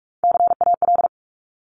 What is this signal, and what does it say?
Morse code, heard as a keyed tone, about 46 wpm, 700 Hz. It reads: CAL
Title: Top Three Letter Combinations in Words 45wpm